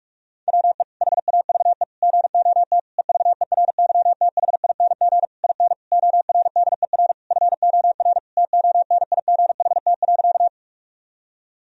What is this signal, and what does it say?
Morse code, heard as a keyed tone, about 46 wpm, 700 Hz. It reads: WE HAVE GOT EVERYTHING IN ORDER FOR TONIGHT.
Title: Sentences from Top 1000 Words - Review Part 3 - 45wpm